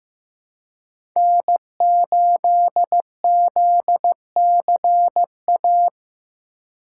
Morse code, heard as a keyed tone, about 15 wpm, 700 Hz. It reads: N8ZCA
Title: USA Call Signs - 5 Characters - 3x Character Spacing - 15wpm